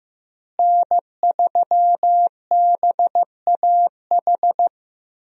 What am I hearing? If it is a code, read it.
Morse code, heard as a keyed tone, about 15 wpm, 700 Hz. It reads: N3BAH
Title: USA Call Signs - 5 Characters - Repeated 2x - 15wpm